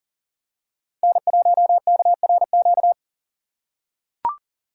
Morse code, heard as a keyed tone, about 40 wpm, 700 Hz. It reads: N1KRQ
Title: USA Call Signs - 5 Characters - 6x Character Spacing - 40wpm